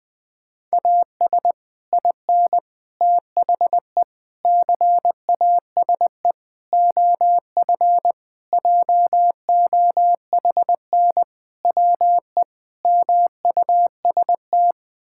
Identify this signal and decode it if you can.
Morse code, heard as a keyed tone, about 20 wpm, 700 Hz. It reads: AS IN THE CASE OF JOHN WE MUST
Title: Sentences with 5-Word Phrases - 20wpm